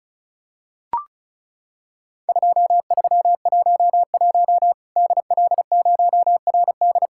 Morse code, heard as a keyed tone, about 34 wpm, 700 Hz. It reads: 2311 DL0RD
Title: Worked All Europe DX Contest - QTC Exchange - 35wpm